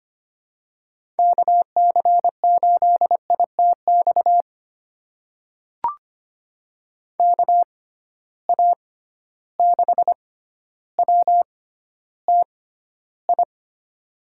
Morse code, heard as a keyed tone, about 25 wpm, spaced as for 13 wpm, 700 Hz. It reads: KC8ITX KA6WTI
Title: USA Call Signs - 6x Character Spacing - 25wpm